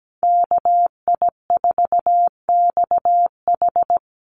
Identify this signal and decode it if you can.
Morse code, heard as a keyed tone, about 17 wpm, 700 Hz. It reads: KI4XH
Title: Call Signs Repeated 3x - 17wpm